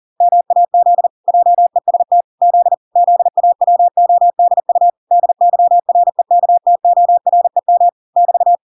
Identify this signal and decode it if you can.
Morse code, heard as a keyed tone, about 40 wpm, 700 Hz. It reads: MAZ JEST Z ZAWODU DYREKTOREM <BT>